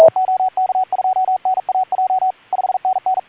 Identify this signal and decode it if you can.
Morse code, heard as a keyed tone, about 42 wpm, 750 Hz, with radio noise. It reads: OK2NAJ 5NN
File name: pile up CUC 130 OK2ZC